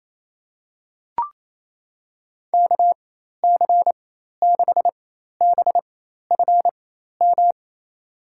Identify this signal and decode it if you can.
Morse code, heard as a keyed tone, about 28 wpm, spaced as for 13 wpm, 700 Hz. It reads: KC6BFM
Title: USA Call Signs - 6 Characters - 4x Character Spacing - 28wpm